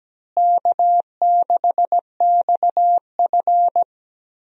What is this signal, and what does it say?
Morse code, heard as a keyed tone, about 17 wpm, 700 Hz. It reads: K6XF
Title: USA Call Signs - 4 Characters - Repeated 2x - 17wpm